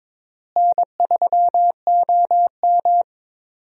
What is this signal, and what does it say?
Morse code, heard as a keyed tone, about 22 wpm, 700 Hz. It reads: N3OM
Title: USA Call Signs - 4 Characters - Repeated 3x - 22wpm